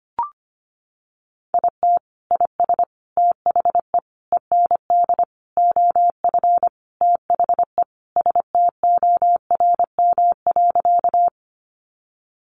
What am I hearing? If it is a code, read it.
Morse code, heard as a keyed tone, about 25 wpm, 700 Hz. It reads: IT IS THE END OF THE STORM.
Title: Sentences from Top 1500 Words 25wpm